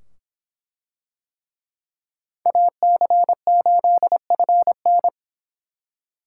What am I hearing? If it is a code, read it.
Morse code, heard as a keyed tone, about 23 wpm, 700 Hz. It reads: AC8FN